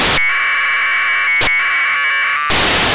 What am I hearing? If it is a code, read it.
Packet radio (AX.25, as in APRS).
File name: antelsat 0843utc